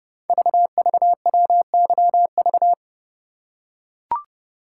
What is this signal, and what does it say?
Morse code, heard as a keyed tone, about 30 wpm, 690 Hz. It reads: VVWYV
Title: Code Groups - 5 Letters - 30wpm